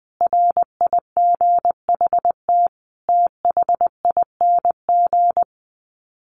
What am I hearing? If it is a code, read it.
Morse code, heard as a keyed tone, about 20 wpm, 700 Hz. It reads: RIGHT THING
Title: Sets of 5 Words 20wpm